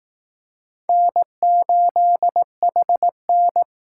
Morse code, heard as a keyed tone, about 19 wpm, 700 Hz. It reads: N8HN